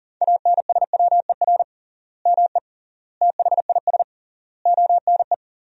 Morse code, heard as a keyed tone, about 40 wpm, 700 Hz. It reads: ANSWER ME THIS ONE
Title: 3x Word-Spacing - Sentences from Top 400 Words - 40wpm